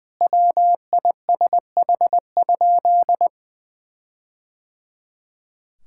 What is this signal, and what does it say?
Morse code, heard as a keyed tone, about 20 wpm, 700 Hz. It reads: WISH?
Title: Sentences from Top 1600 Words 20wpm